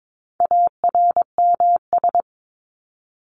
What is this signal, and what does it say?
Morse code, heard as a keyed tone, about 22 wpm, 700 Hz. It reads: ARMS
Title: ES:S Suffix - Top 100 Words with Base Words 22wpm